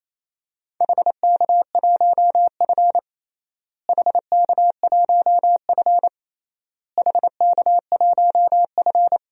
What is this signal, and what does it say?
Morse code, heard as a keyed tone, about 27 wpm, 700 Hz. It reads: HK1F HK1F HK1F
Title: DXCC-100 Call Signs - 4 Characters - Repeated 3x - 28wpm